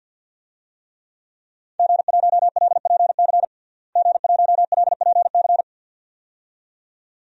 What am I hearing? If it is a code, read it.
Morse code, heard as a keyed tone, about 47 wpm, 700 Hz. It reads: G1LPC G1LPC
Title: DXCC-100 Call Signs - 5 Characters - Repeated 2x - 50wpm